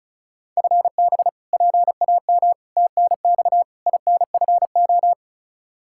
Morse code, heard as a keyed tone, about 36 wpm, 700 Hz. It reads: FB PAM TNX INFO
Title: K1USN SST QSO Practice 35wpm